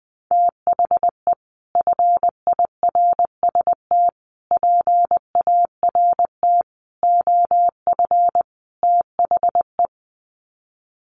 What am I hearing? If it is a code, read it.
Morse code, heard as a keyed tone, about 19 wpm, 700 Hz. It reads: THE FIRST PART OF THE
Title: Sets of 5 Words Encore 20wpm